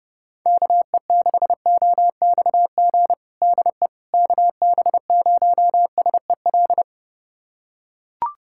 Morse code, heard as a keyed tone, about 30 wpm, 700 Hz. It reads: KE6OXG DE KB0SEL
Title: USA Callsign DE Callsign - 6 Characters - 30wpm